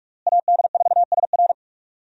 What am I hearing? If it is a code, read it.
Morse code, heard as a keyed tone, about 41 wpm, 700 Hz. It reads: AD4SR
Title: USA Call Signs - 5 Characters - Repeated 2x - 45wpm